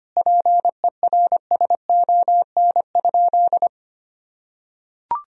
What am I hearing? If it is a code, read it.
Morse code, heard as a keyed tone, about 25 wpm, 700 Hz. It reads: PERSON?
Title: Sentences from Top 1000 Words - Review Part 4 - 25wpm